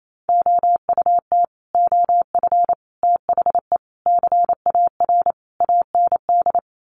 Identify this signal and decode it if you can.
Morse code, heard as a keyed tone, about 28 wpm, 700 Hz. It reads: OUT OF THE CAR AND
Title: Sets of 5 Words Encore 28wpm